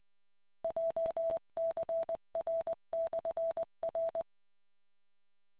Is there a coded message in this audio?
Yes: Morse code from 0.6 to 4.2 s, heard as a keyed tone, about 20 wpm, 660 Hz. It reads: JCR/R